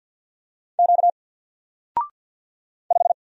Morse code, heard as a keyed tone, about 53 wpm, 700 Hz. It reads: Q 5
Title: Single Letter-Number Alt 50wpm